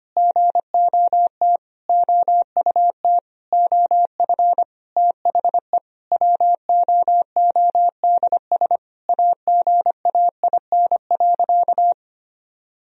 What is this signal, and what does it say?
Morse code, heard as a keyed tone, about 25 wpm, 700 Hz. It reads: GOT OUT OF THE WOODS AGAIN.
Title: Sentences from Top 1000 Words - Review Part 2 - 25wpm